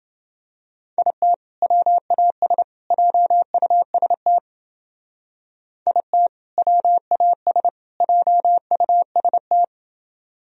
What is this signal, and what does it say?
Morse code, heard as a keyed tone, about 28 wpm, 700 Hz. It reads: IT WAS JUST IT WAS JUST